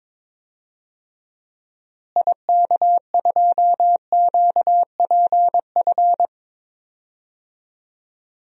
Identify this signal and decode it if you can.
Morse code, heard as a keyed tone, about 22 wpm, 700 Hz. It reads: IK2QPF
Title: DXCC-100 Call Signs - 6 Characters - Repeated 2x - 22wpm